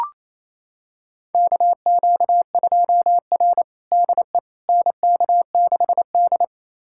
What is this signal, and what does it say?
Morse code, heard as a keyed tone, about 28 wpm, 700 Hz. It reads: KQ2R DE NK6D
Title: USA Callsign DE Callsign - 4 Characters - 28wpm